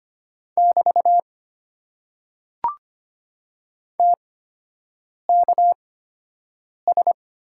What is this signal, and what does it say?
Morse code, heard as a keyed tone, about 25 wpm, spaced as for 7 wpm, 700 Hz. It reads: <BT> TKS
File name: Top 90 QSO Elements - 8x Character Spacing - 25wpm